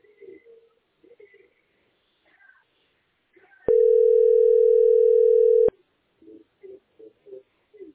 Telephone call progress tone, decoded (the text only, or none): ringback tone